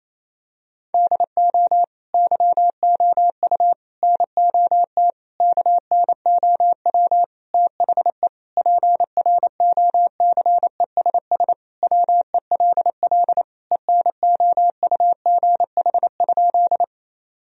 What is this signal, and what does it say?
Morse code, heard as a keyed tone, about 28 wpm, 700 Hz. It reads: DO YOU NOT KNOW THE PROCESS WELL ENOUGH?
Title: Sentences from Top 1000 Words - Review Part 3 - 28wpm